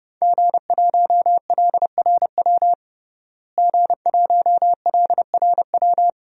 Morse code, heard as a keyed tone, about 32 wpm, 700 Hz. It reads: G1LRW G1LRW
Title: DXCC-100 Call Signs - 5 Characters - Repeated 3x - 30wpm